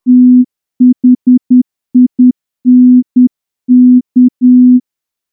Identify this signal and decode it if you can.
Morse code, heard as a keyed tone, about 10 wpm, 250 Hz. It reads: THINK
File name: morse-training-10WPM